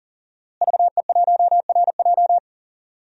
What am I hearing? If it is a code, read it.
Morse code, heard as a keyed tone, about 40 wpm, 700 Hz. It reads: VE1RJ
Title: DXCC-100 Call Signs - 5 Characters - Repeated 3x - 40wpm